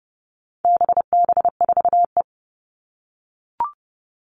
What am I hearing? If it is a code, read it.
Morse code, heard as a keyed tone, about 30 wpm, 700 Hz. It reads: BB4E
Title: All Cut Numbers - 4-Digit 30wpm